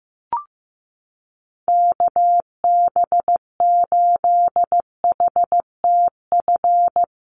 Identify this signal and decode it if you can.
Morse code, heard as a keyed tone, about 15 wpm, 700 Hz. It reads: KB8HTF
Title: USA Call Signs 15wpm